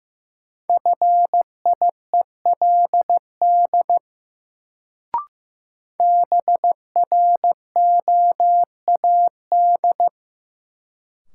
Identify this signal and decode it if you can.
Morse code, heard as a keyed tone, about 15 wpm, 700 Hz. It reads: FIELD BROAD
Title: Sentences from Top 1000 Words - New Word Drill - 15wpm